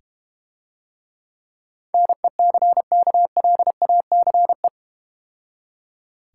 Morse code, heard as a keyed tone, about 31 wpm, 700 Hz. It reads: NECKLACE
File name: practice / 25wpm / Difficult Words - Speed-Racing with Hints - 25wpm